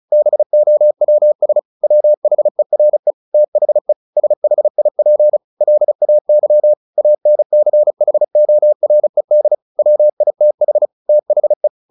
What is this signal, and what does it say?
Morse code, heard as a keyed tone, about 35 wpm, 600 Hz. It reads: BOWS WHERE THE SHIP LAY ANCHORED WITH THE
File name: the old man and the sea 2 35wpm